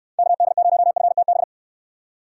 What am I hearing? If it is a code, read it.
Morse code, heard as a keyed tone, about 69 wpm, 700 Hz. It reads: KD0FTB